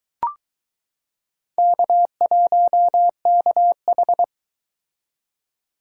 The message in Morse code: K1KH

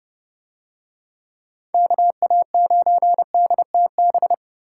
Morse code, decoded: KA9DTB

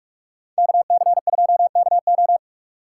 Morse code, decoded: KC2KO

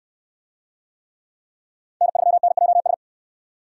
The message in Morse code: N3NJS

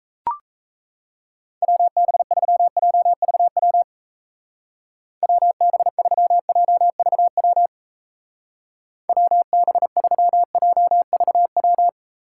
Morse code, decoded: WB3JVW WB3JVW WB3JVW

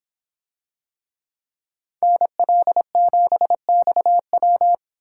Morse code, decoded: NL7XW

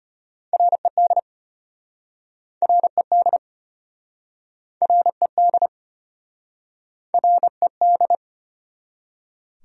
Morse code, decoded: RED RED RED RED

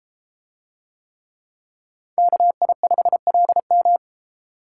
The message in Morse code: KI5LM

